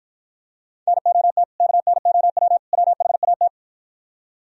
Morse code, heard as a keyed tone, about 52 wpm, 690 Hz. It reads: NOT KNOW WHAT